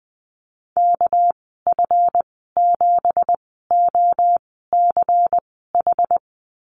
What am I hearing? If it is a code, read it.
Morse code, heard as a keyed tone, about 20 wpm, spaced as for 15 wpm, 700 Hz. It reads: KF7OCH
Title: 207 Lesson 33g - Letter j - Call Signs 20wpm